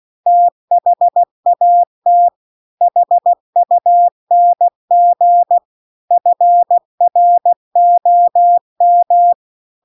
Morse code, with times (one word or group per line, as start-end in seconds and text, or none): 0.3-2.3 THAT
2.8-5.6 HUNG
6.1-9.3 FROM